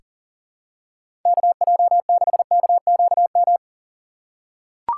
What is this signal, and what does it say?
Morse code, heard as a keyed tone, about 39 wpm, 700 Hz. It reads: KJ6KQM
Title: USA Call Signs - 4x Character Spacing - 40wpm